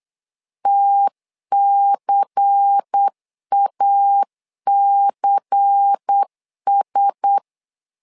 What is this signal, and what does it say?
Morse code, heard as a keyed tone, about 8 wpm, 800 Hz. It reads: TCACS
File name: LEZIONI / 5 LBVC / 5 LBVC 8wpm 2935